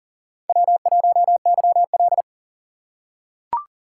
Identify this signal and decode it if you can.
Morse code, heard as a keyed tone, about 40 wpm, 700 Hz. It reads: W1YL
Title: USA Call Signs - 3x Character Spacing - 40wpm